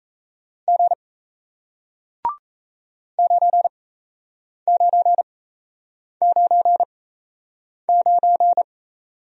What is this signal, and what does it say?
Morse code, heard as a keyed tone, about 38 wpm, spaced as for 8 wpm, 700 Hz. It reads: G 9999